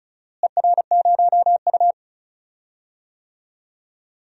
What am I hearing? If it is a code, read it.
Morse code, heard as a keyed tone, about 34 wpm, 700 Hz. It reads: ER0U
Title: DXCC-100 Call Signs - 4 Characters - Repeated 2x - 35wpm